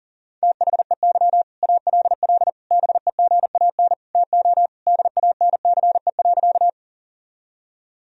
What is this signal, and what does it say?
Morse code, heard as a keyed tone, about 40 wpm, 700 Hz. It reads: THEY ALL BEGAN TO DANCE.
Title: Sentences from Top 1000 Words - Review Part 4 - 40wpm